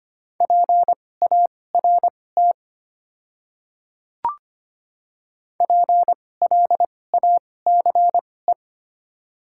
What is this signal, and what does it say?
Morse code, heard as a keyed tone, about 25 wpm, spaced as for 18 wpm, 700 Hz. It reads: PART PLACE
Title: 2x Character Spacing - Top 300 Words - 25wpm